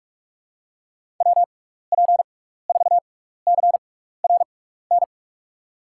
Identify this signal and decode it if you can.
Morse code, heard as a keyed tone, about 44 wpm, spaced as for 15 wpm, 700 Hz. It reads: WP4CRN
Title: USA Call Signs - 6 Characters - 6x Character Spacing - 45wpm